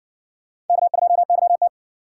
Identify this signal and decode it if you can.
Morse code, heard as a keyed tone, about 60 wpm, 700 Hz. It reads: K2YT